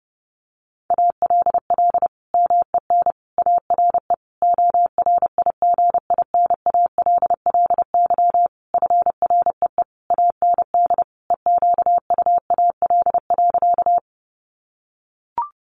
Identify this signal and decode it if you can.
Morse code, heard as a keyed tone, about 30 wpm, 700 Hz. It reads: ALL MEN ARE ORIGINALLY FREE AND EQUAL.
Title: Sentences from Top 1500 Words 30wpm